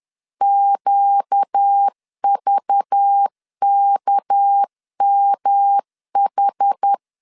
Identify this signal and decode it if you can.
Morse code, heard as a keyed tone, about 10 wpm, 800 Hz. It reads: QVKMH
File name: LEZIONI / 6 PKX0 / 6 PKX0 10wpm 2707